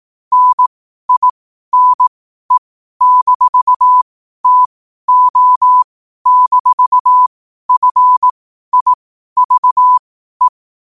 Morse code, heard as a keyed tone, about 18 wpm, spaced as for 12 wpm, 1000 Hz. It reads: NINE-TO-FIVE
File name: 12wpm